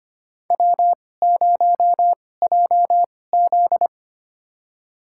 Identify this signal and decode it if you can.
Morse code, heard as a keyed tone, about 25 wpm, spaced as for 18 wpm, 700 Hz. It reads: W0JZ
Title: USA Call Signs - 4 Characters - 2x Character Spacing - 25wpm